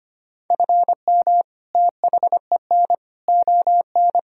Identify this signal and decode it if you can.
Morse code, heard as a keyed tone, about 25 wpm, 700 Hz. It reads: FM THEN ON
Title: Sentences from Top 400 Words and Abbreviations 25wpm